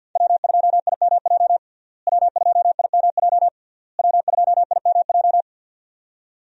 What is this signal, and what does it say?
Morse code, heard as a keyed tone, about 52 wpm, 700 Hz. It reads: W2IMJ W2IMJ W2IMJ
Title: USA Call Signs - 5 Characters - Repeated 3x - 50wpm